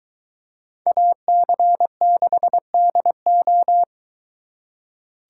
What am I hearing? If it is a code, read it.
Morse code, heard as a keyed tone, about 23 wpm, 700 Hz. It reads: AC6DO